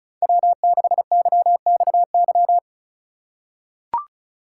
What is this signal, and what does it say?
Morse code, heard as a keyed tone, about 34 wpm, 700 Hz. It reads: W6YXY
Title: USA Call Signs - 5 Characters - 3x Character Spacing - 35wpm